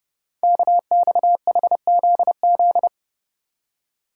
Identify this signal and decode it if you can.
Morse code, heard as a keyed tone, about 28 wpm, 700 Hz. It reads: KXHZZ
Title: Code Groups - 5 Letters - 30wpm